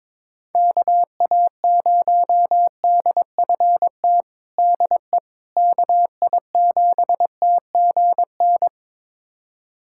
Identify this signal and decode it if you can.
Morse code, heard as a keyed tone, about 21 wpm, 700 Hz. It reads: KA0DFT DE KI7TGN
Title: USA Callsign DE Callsign - 6 Characters - 22wpm